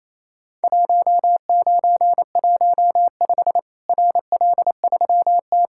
Morse code, heard as a keyed tone, about 28 wpm, 700 Hz. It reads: 1915 RL3T
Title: Worked All Europe DX Contest - QTC Exchange - 28wpm